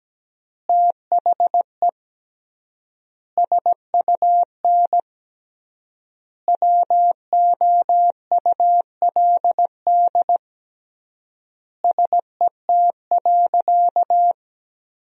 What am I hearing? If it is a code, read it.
Morse code, heard as a keyed tone, about 17 wpm, 700 Hz. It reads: THE SUN WOULD SET.
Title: 3x Word-Spacing - Sentences from Top 400 Words - 17wpm